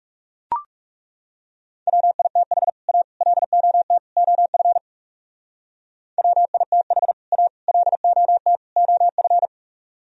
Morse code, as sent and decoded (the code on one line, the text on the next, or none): .-- .. - .... / .- / .-.. --- - / --- ..-. / .-- .. - .... / .- / .-.. --- - / --- ..-.
WITH A LOT OF WITH A LOT OF